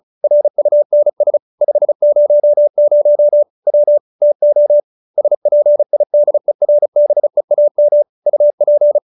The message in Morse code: RUNS 500 W TO SPIDERBEAM UP

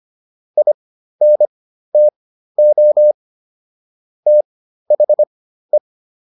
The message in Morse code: INTO THE

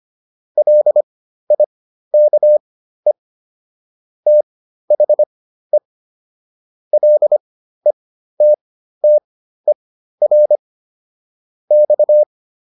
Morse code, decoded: LIKE THE LETTER X

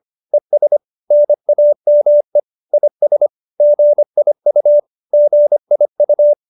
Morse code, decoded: ES NAME IS GIU GIU

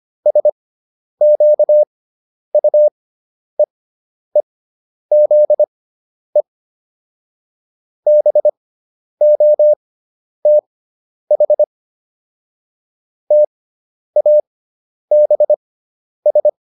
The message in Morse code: SQUEEZE BOTH TABS

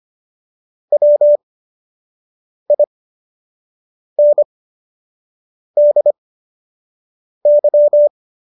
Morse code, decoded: WINDY

